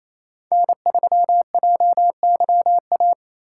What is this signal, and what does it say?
Morse code, heard as a keyed tone, about 28 wpm, 700 Hz. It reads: N3JYA
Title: USA Call Signs - 5 Characters - 2x Character Spacing - 28wpm